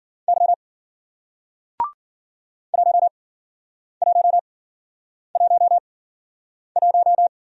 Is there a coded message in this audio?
Yes: Morse code from 0.3 to 7.3 s, heard as a keyed tone, about 54 wpm, spaced as for 9 wpm, 700 Hz. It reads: Y 1111